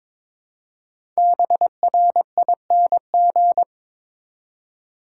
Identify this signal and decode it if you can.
Morse code, heard as a keyed tone, about 22 wpm, 700 Hz. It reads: BRING
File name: 5 Letter Words - First 3 Letters Given - 22wpm